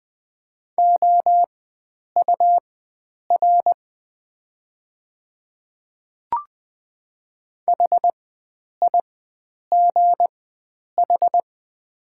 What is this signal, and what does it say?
Morse code, heard as a keyed tone, about 20 wpm, spaced as for 9 wpm, 700 Hz. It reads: OUR HIGH
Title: 4x Character Spacing - Top 300 Words - 20wpm